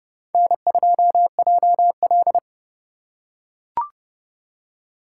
Morse code, heard as a keyed tone, about 28 wpm, 700 Hz. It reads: N2JL